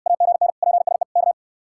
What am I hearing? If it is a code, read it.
Morse code, heard as a keyed tone, about 68 wpm, 700 Hz. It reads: AGN PSE K